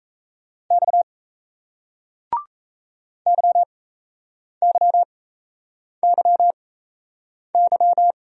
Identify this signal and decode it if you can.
Morse code, heard as a keyed tone, about 40 wpm, spaced as for 8 wpm, 700 Hz. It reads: X YYYY